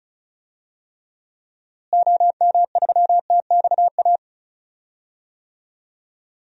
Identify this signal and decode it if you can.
Morse code, heard as a keyed tone, about 36 wpm, 700 Hz. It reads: OM3TXA
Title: DXCC-100 Call Signs - 6 Characters - Repeated 2x - 35wpm